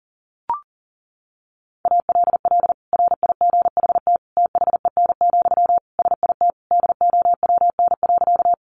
Morse code, heard as a keyed tone, about 40 wpm, 700 Hz. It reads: ALL RIGHT THEN, SIT DOWN.
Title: Sentences from Top 500 Words 40wpm